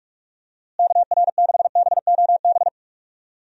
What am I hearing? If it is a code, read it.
Morse code, heard as a keyed tone, about 46 wpm, 700 Hz. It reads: KR6BOB
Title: USA Call Signs - 6 Characters - Repeated 3x - 45wpm